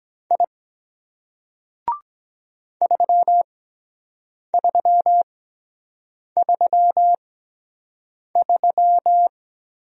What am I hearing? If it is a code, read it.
Morse code, heard as a keyed tone, about 24 wpm, spaced as for 7 wpm, 700 Hz. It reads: I 3333